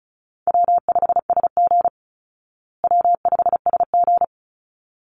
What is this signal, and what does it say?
Morse code, heard as a keyed tone, about 34 wpm, 700 Hz. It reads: W5SG W5SG
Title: Easy Call Signs Repeated 2x - 35wpm